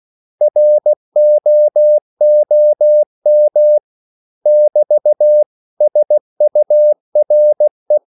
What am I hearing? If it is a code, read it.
Morse code, heard as a keyed tone, about 16 wpm, 600 Hz. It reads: ROOM <BT> SURE
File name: CW / Practice / 500MostCommonEnglishWords16